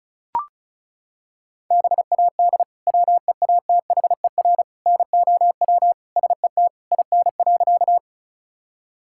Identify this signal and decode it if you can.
Morse code, heard as a keyed tone, about 34 wpm, 690 Hz. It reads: BAD WEATHER NOW SET IN.
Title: Sentences from Top 1000 Words - Review Part 5 - 35wpm